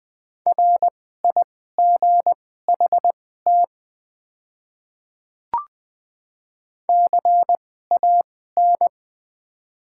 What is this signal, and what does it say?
Morse code, heard as a keyed tone, about 20 wpm, spaced as for 14 wpm, 700 Hz. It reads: RIGHT CAN
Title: 2x Character Spacing - Top 200 Words - 20wpm